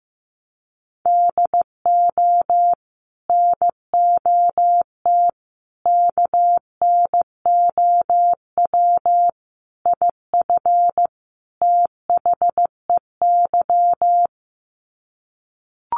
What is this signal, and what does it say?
Morse code, heard as a keyed tone, about 15 wpm, 700 Hz. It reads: DO NOT KNOW IF THEY
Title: Sets of 5 Words Encore 15wpm